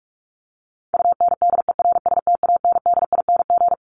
Morse code, heard as a keyed tone, about 44 wpm, 700 Hz. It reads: UNDERSTANDING
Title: ING Suffix - Top 100 Words with Base Words 45wpm